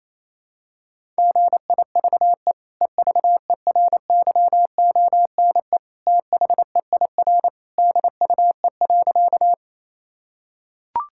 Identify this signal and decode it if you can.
Morse code, heard as a keyed tone, about 28 wpm, 700 Hz. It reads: GIVE EVERYONE THEIR DUE.
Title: Sentences from Top 1000 Words - Review Part 6 - 28wpm